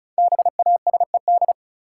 Morse code, heard as a keyed tone, about 34 wpm, 700 Hz. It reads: BASED
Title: Sets of 2 Words - New Word Drill - 35wpm